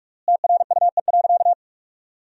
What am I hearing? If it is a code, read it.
Morse code, heard as a keyed tone, about 46 wpm, 700 Hz. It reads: TRUE.
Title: 3x Word-Spacing - Sentences from Top 400 Words - 45wpm